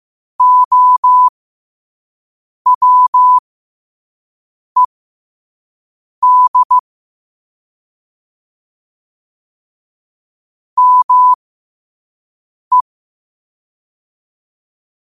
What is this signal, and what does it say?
Morse code, heard as a keyed tone, about 15 wpm, spaced as for 5 wpm, 1000 Hz. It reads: OWED ME
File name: qotdfiles / 5wpm